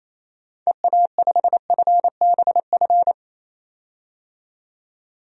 Morse code, heard as a keyed tone, about 28 wpm, 700 Hz. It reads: EA5FBF
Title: DXCC-100 Call Signs - 6 Characters - 28wpm